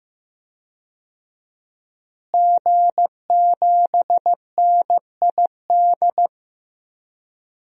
Morse code, heard as a keyed tone, about 15 wpm, 700 Hz. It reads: G7NID
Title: DXCC-100 Call Signs - 5 Characters - Repeated 3x - 15wpm